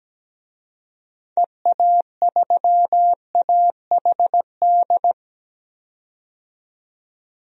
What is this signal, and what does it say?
Morse code, heard as a keyed tone, about 17 wpm, 700 Hz. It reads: EA3AHD
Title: DXCC-100 Call Signs - 6 Characters - 17wpm